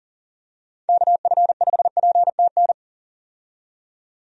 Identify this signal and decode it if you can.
Morse code, heard as a keyed tone, about 40 wpm, 700 Hz. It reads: KF5PTN
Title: USA Call Signs - 6 Characters - 40wpm